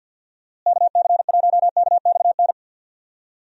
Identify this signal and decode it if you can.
Morse code, heard as a keyed tone, about 46 wpm, 700 Hz. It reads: KC1KXN
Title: USA Call Signs - 6 Characters - Repeated 2x - 50wpm